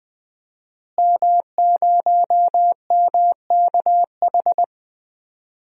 Morse code, heard as a keyed tone, about 20 wpm, 700 Hz. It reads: M0MKH
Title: DXCC-100 Call Signs - 5 Characters - Repeated 2x - 20wpm